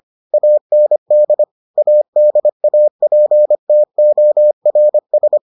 Morse code, heard as a keyed tone, about 25 wpm, 600 Hz. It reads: AND ADAPTORS